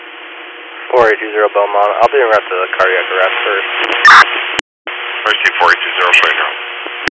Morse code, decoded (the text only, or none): none